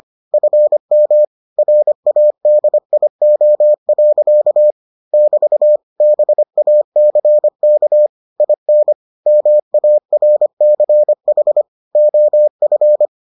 Morse code, heard as a keyed tone, about 25 wpm, 600 Hz. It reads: FM RADIO. <BT> BACK IN MARCH OF